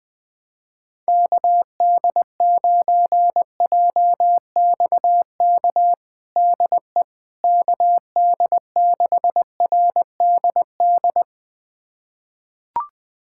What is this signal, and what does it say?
Morse code, heard as a keyed tone, about 20 wpm, 700 Hz. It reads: KD9JXK DE KD6RDD
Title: USA Callsign DE Callsign - 6 Characters - 20wpm